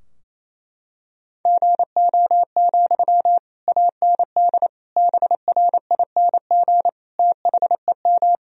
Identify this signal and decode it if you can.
Morse code, heard as a keyed tone, about 28 wpm, 700 Hz. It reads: GO, AND BRING THEM
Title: Sentences from Top 1000 Words - Review Part 3 - 28wpm